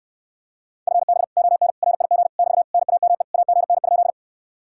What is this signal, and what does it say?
Morse code, heard as a keyed tone, about 67 wpm, 700 Hz. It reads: FB ON RIG <BT> NAME AGN?